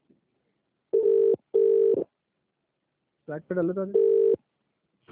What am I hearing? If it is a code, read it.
Ringback tone (UK), 400 + 450 Hz, with speech.